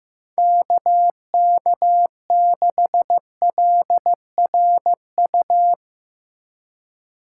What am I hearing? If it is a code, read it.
Morse code, heard as a keyed tone, about 15 wpm, 700 Hz. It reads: KK6LRU
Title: USA Call Signs - 6 Characters - 2x Character Spacing - 15wpm